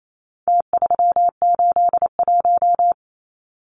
Morse code, heard as a keyed tone, about 28 wpm, 700 Hz. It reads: T381
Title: Numbers - Contest Style - 4-Digit 28wpm